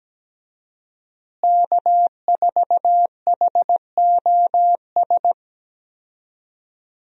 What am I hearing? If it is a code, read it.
Morse code, heard as a keyed tone, about 17 wpm, 700 Hz. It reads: K4HOS
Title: USA Call Signs - 5 Characters - Repeated 3x - 17wpm